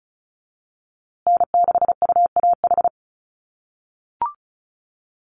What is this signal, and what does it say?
Morse code, heard as a keyed tone, about 34 wpm, 700 Hz. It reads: N6UAH
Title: USA Call Signs Repeated 2x 35wpm